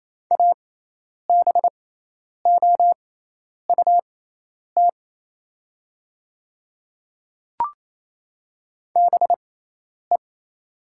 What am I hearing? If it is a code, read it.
Morse code, heard as a keyed tone, about 27 wpm, spaced as for 10 wpm, 700 Hz. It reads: ABOUT BE
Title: 6x Character Spacing - Top 100 Words - 28wpm